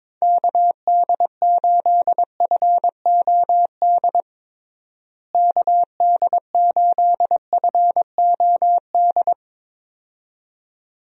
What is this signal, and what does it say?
Morse code, heard as a keyed tone, about 22 wpm, 700 Hz. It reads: KD8FOD KD8FOD
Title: USA Call Signs - 6 Characters - Repeated 2x - 22wpm